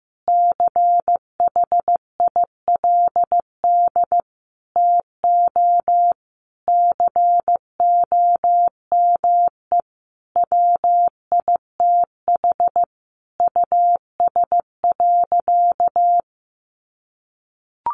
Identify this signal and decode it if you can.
Morse code, heard as a keyed tone, about 15 wpm, 700 Hz. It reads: CHILD TO COME WITH US.
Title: Sentences from Top 1300 Words 15wpm